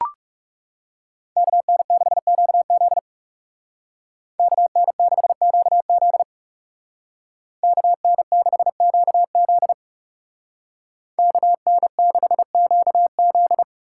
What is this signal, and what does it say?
Morse code, heard as a keyed tone, about 40 wpm, 700 Hz. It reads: KN6QZ KN6QZ KN6QZ KN6QZ